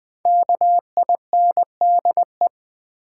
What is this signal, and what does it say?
Morse code, heard as a keyed tone, about 20 wpm, 700 Hz. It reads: KINDE
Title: Build A Word - 5 Letters At A Time - 12-Letter Words - 20wpm